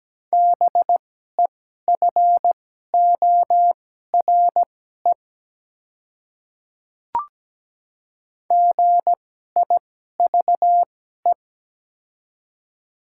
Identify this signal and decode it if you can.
Morse code, heard as a keyed tone, about 17 wpm, spaced as for 12 wpm, 700 Hz. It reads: BEFORE GIVE
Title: 2x Character Spacing - Top 200 Words - 17wpm